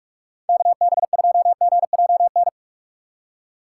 Morse code, heard as a keyed tone, about 46 wpm, 700 Hz. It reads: KB2GJN
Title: USA Call Signs - 6 Characters - 3x Character Spacing - 45wpm